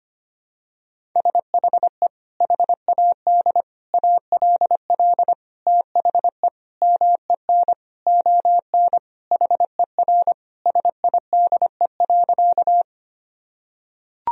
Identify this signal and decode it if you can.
Morse code, heard as a keyed tone, about 25 wpm, 700 Hz. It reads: SHE HAD ALL THE MEN ON HER SIDE.
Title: Sentences from Top 1000 Words - Review Part 1 - 25wpm